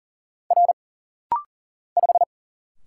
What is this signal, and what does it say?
Morse code, heard as a keyed tone, about 40 wpm, 700 Hz. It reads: R 5
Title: Single Letter-Number - Rapid-Fire 40wpm